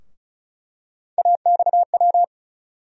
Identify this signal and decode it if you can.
Morse code, heard as a keyed tone, about 36 wpm, 700 Hz. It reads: AXW
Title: USA Call Sign Postfixes - 3 Letters 35wpm